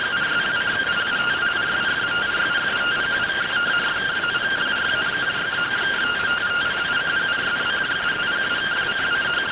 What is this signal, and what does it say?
Two-tone teleprinter signal, about 170 Hz shift.